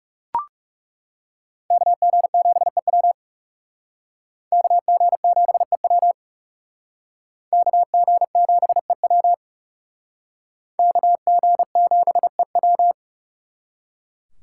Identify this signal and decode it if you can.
Morse code, heard as a keyed tone, about 39 wpm, 700 Hz. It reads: KG7EW KG7EW KG7EW KG7EW